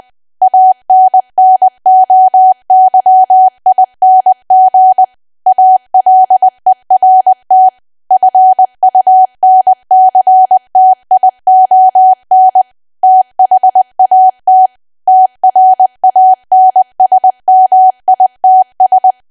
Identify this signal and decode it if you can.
Morse code, heard as a keyed tone, about 20 wpm, 750 Hz. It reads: ANNOYING ALERT FUNCTION THAT TRANSMITS
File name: Archive / 20 WPM / 20WPM